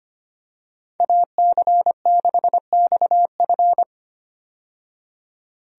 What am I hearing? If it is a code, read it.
Morse code, heard as a keyed tone, about 25 wpm, 700 Hz. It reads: AC6XF
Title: USA Call Signs - 5 Characters - 25wpm